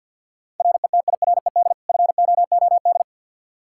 Morse code, heard as a keyed tone, about 46 wpm, 700 Hz. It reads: RETIRED FOOD
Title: Practice QSO - Exchange 3 - More Info - 50wpm